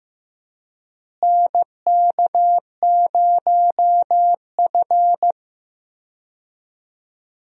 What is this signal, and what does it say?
Morse code, heard as a keyed tone, about 15 wpm, 700 Hz. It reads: NK0F
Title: USA Call Signs - 4 Characters - Repeated 3x - 15wpm